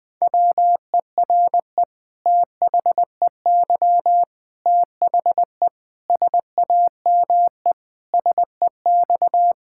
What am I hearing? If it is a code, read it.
Morse code, heard as a keyed tone, about 20 wpm, 700 Hz. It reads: WERE THEY THE SAME SEX
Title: Sentences from Top 1600 Words 20wpm